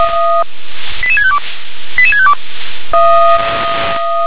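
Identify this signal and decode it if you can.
Five-tone selective call (ZVEI), heard as quick runs of five tones, tones of about 70 ms. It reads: 88652 88652
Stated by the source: ZVEI five-tone call received with an SDR.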